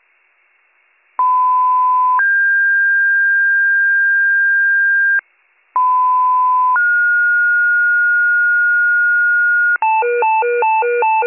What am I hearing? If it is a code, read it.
Two-tone page, heard as a one-second tone, then a three-second tone, about 1010 then 1600 Hz, about 1010 then 1430 Hz, with noise. It is heard from 1.2 to 9.8 s.